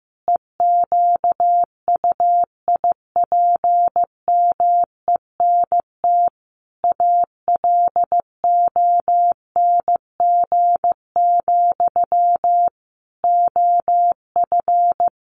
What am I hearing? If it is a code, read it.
Morse code, heard as a keyed tone, about 15 wpm, 700 Hz. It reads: EQUIPMENT ALONG, OF
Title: Sentences from Top 700 Words 15wpm